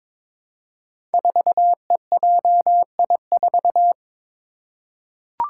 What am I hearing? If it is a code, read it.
Morse code, heard as a keyed tone, about 22 wpm, 700 Hz. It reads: 4EJI4
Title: Five Challenging Letter-Number Combinations 22wpm